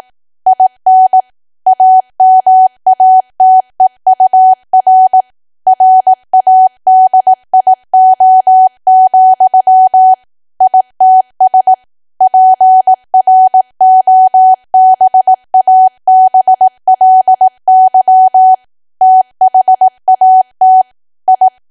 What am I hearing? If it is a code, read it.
Morse code, heard as a keyed tone, about 18 wpm, 750 Hz. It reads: IN AMATEUR RADIO, ITS PROBABLY THAT I
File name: Archive / 18 WPM / 18WPM